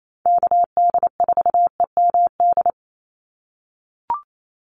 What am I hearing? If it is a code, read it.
Morse code, heard as a keyed tone, about 28 wpm, 700 Hz. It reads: KD4EMD
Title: USA Call Signs 28wpm